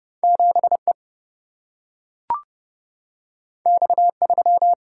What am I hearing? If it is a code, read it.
Morse code, heard as a keyed tone, about 29 wpm, 700 Hz. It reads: 7E X3
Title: Single-Letter Single-Number 30wpm